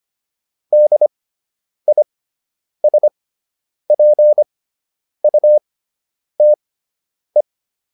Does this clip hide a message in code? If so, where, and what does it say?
Yes: Morse code from 0.7 to 7.4 s, heard as a keyed tone, about 25 wpm, spaced as for 9 wpm, 600 Hz. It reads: DISPUTE